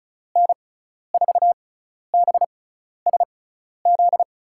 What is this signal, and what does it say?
Morse code, heard as a keyed tone, about 36 wpm, spaced as for 12 wpm, 700 Hz. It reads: N4BSZ
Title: USA Call Signs - 6 Characters - 6x Character Spacing - 35wpm